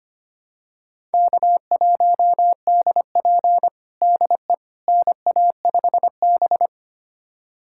Morse code, heard as a keyed tone, about 25 wpm, 700 Hz. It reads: K1DP DE NA5B
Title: USA Callsign DE Callsign - 4 Characters - 25wpm